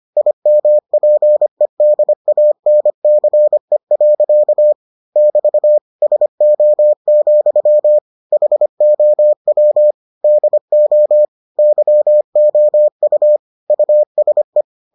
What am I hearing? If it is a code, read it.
Morse code, heard as a keyed tone, about 25 wpm, 600 Hz. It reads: IMPEDANCE. <BT> SO, HOW DO YOU USE